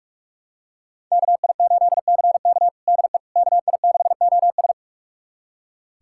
Morse code, heard as a keyed tone, about 46 wpm, 700 Hz. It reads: KI8CK DE KI6OS
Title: USA Callsign DE Callsign - 5 Characters - 45wpm